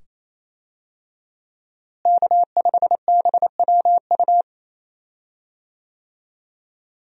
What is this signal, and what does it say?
Morse code, heard as a keyed tone, about 28 wpm, 700 Hz. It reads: K5BWU
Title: USA Call Signs - 5 Characters - Repeated 2x - 28wpm